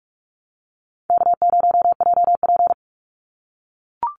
Morse code, heard as a keyed tone, about 44 wpm, 700 Hz. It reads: K0JP
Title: Easy Call Signs - 45wpm